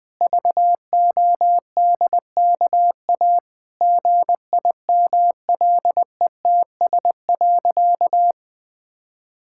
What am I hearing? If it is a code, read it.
Morse code, heard as a keyed tone, about 20 wpm, 700 Hz. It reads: VODKA GIMLETS.
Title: Pangram Challenge v2 - 20wpm